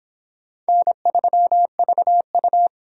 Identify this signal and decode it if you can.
Morse code, heard as a keyed tone, about 24 wpm, 700 Hz. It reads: N3VU